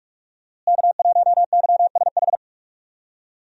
Morse code, heard as a keyed tone, about 46 wpm, 700 Hz. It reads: K1YSH
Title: USA Call Signs - 5 Characters - Repeated 2x - 45wpm